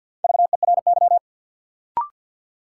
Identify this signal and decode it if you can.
Morse code, heard as a keyed tone, about 50 wpm, 700 Hz. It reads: VERY
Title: Sentences from Top 200 Words - New Word Drill - 50wpm